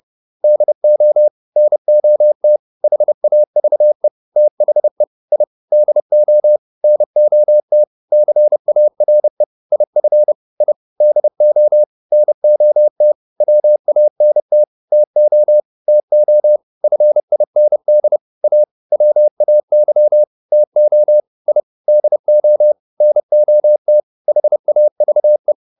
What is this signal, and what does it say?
Morse code, heard as a keyed tone, about 30 wpm, 600 Hz. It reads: DO NOT HAVE THE I DO NOT CARE IF I DO NOT WANT TO TO FIND A WAY TO I DO NOT HAVE